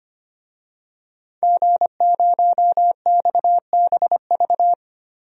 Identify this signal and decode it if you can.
Morse code, heard as a keyed tone, about 25 wpm, 700 Hz. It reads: G0XBV
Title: DXCC-100 Call Signs - 5 Characters - 25wpm